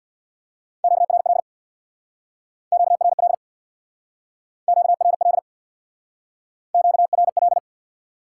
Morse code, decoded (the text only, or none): QRL QRL QRL QRL